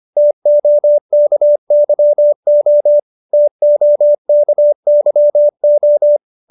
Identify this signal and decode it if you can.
Morse code, heard as a keyed tone, about 25 wpm, 590 Hz. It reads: TOKYO TOKYO